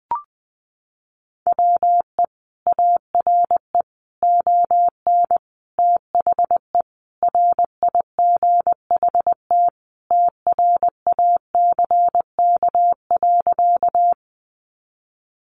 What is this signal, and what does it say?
Morse code, heard as a keyed tone, about 20 wpm, 700 Hz. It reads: WE ARE ON THE RIGHT TRACK.
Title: Sentences from Top 1100 Words 20wpm